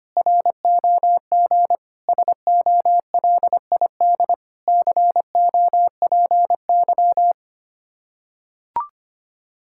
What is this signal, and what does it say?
Morse code, heard as a keyed tone, about 25 wpm, 700 Hz. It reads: ROG SOLID COPY
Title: Practice QSO - Exchange 3 - More Info - 25wpm